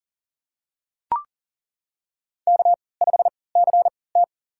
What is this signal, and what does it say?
Morse code, heard as a keyed tone, about 37 wpm, spaced as for 22 wpm, 700 Hz. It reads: K5CT
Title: USA Call Signs - 4 Characters - 3x Character Spacing - 40wpm